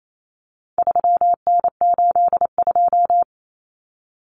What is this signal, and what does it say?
Morse code, heard as a keyed tone, about 28 wpm, 700 Hz. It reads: 3N82
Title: Common Cut Numbers - 4-Digit 28wpm